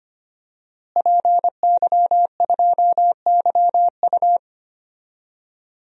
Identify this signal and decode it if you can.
Morse code, heard as a keyed tone, about 25 wpm, 700 Hz. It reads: PY2YU
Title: DXCC-100 Call Signs - 5 Characters - Repeated 3x - 25wpm